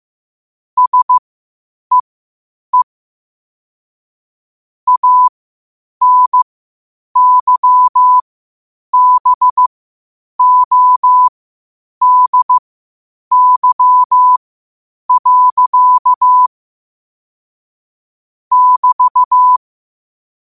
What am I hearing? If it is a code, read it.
Morse code, heard as a keyed tone, about 15 wpm, spaced as for 8 wpm, 1000 Hz. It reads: SEE ANYBODY. <BT>
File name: qotdfiles / 8wpm